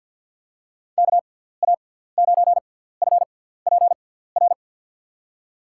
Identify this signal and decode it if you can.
Morse code, heard as a keyed tone, about 52 wpm, 700 Hz. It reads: K A 9 F P R
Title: USA Call Signs - 6 Characters - 6x Character Spacing - 50wpm